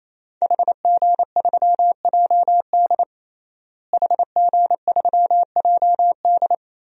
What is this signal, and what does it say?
Morse code, heard as a keyed tone, about 28 wpm, 700 Hz. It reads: HG3JD HG3JD
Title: DXCC-100 Call Signs - 5 Characters - Repeated 3x - 28wpm